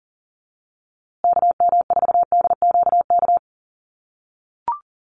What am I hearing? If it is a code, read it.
Morse code, heard as a keyed tone, about 37 wpm, 700 Hz. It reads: KM4DQK
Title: USA Call Signs Repeated 2x 40wpm